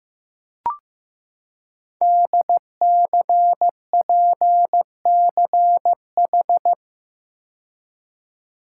Morse code, heard as a keyed tone, about 15 wpm, 700 Hz. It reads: DCPCH
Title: Code Groups - 5 Letters - 15wpm